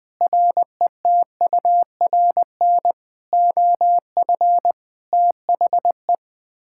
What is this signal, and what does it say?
Morse code, heard as a keyed tone, about 20 wpm, 700 Hz. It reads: RETURN OF THE
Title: Sets of 9 Words - 20wpm